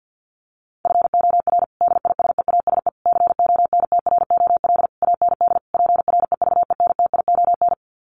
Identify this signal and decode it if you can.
Morse code, heard as a keyed tone, about 50 wpm, 700 Hz. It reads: FOR DISEASE CONTROL AND PREVENTION
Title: Sets of 5 Words Encore2 50wpm